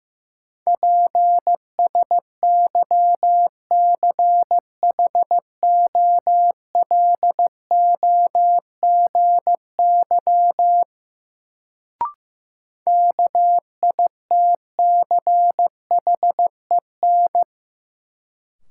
Morse code, heard as a keyed tone, about 15 wpm, 700 Hz. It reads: PSYCHOLOGY KITCHEN
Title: Sentences from Top 1400 Words - New Word Drill - 15wpm